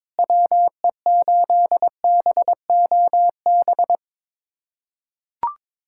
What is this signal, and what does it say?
Morse code, heard as a keyed tone, about 22 wpm, 700 Hz. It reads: WE8BOB
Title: USA Callsign DE Callsign 22wpm